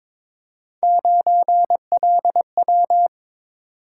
Morse code, heard as a keyed tone, about 22 wpm, 700 Hz. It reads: 9LW
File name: practice / 22wpm / Three Letter-Number Combinations - 22wpm